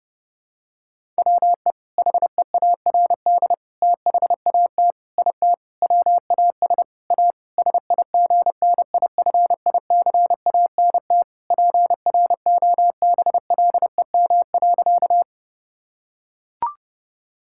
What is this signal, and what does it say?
Morse code, heard as a keyed tone, about 30 wpm, 700 Hz. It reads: WE HEARD THAT IT WAS A SIGNIFICANT PROBLEM.
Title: Sentences from Top 1000 Words - Review Part 6 - 30wpm